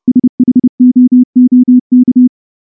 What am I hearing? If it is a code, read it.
Morse code, heard as a keyed tone, about 28 wpm, 250 Hz. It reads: SHOOK
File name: morse / morse-training-30WPM